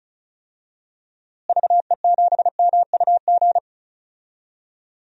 Morse code, heard as a keyed tone, about 36 wpm, 700 Hz. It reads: VE7MUG
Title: DXCC-100 Call Signs - 6 Characters - Repeated 3x - 35wpm